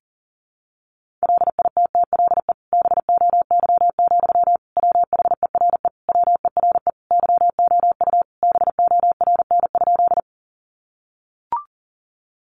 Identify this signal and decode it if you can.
Morse code, heard as a keyed tone, about 40 wpm, 700 Hz. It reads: LITTLE BOY, WHERE WERE YOU BORN?
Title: Sentences from Top 1000 Words 40wpm